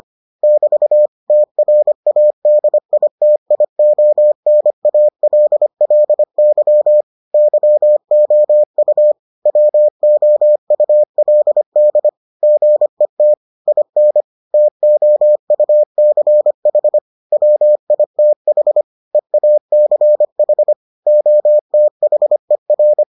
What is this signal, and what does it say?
Morse code, heard as a keyed tone, about 25 wpm, 600 Hz. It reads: <BT> TRADITIONALLY YOU WOULD GET IN TOUCH WITH EACH OTHER